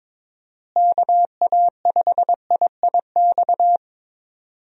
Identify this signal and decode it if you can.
Morse code, heard as a keyed tone, about 22 wpm, 700 Hz. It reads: KA5IIX
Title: USA Call Signs - 6 Characters - Repeated 3x - 22wpm